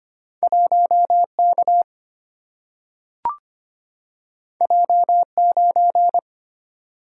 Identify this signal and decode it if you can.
Morse code, heard as a keyed tone, about 25 wpm, 700 Hz. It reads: 1 K J 9
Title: Single-Letter Single-Number 25wpm